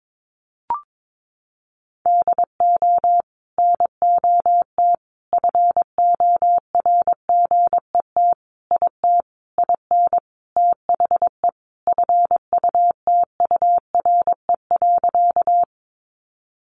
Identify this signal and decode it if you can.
Morse code, heard as a keyed tone, about 22 wpm, 700 Hz. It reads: DO NOT FORGET IT IN THE FUTURE.
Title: Sentences from Top 1200 Words 22wpm